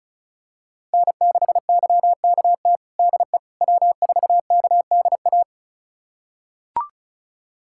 Morse code, heard as a keyed tone, about 34 wpm, 700 Hz. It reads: N6YKT DE W4KDA
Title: USA Callsign DE Callsign - 5 Characters - 35wpm